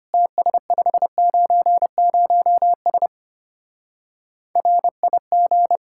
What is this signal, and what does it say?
Morse code, heard as a keyed tone, about 31 wpm, 700 Hz. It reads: TS590S RIG
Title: Speed-Racing Common QSO Phrases - 25wpm